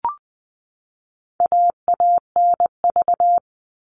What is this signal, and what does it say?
Morse code, heard as a keyed tone, about 20 wpm, 700 Hz. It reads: AANV
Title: All Cut Numbers - 4-Digit 20wpm